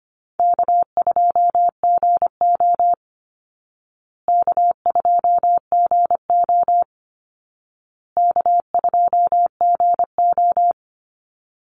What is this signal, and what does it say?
Morse code, heard as a keyed tone, about 25 wpm, 700 Hz. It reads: K2GO K2GO K2GO
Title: Easy Call Signs Repeated 3x - 25wpm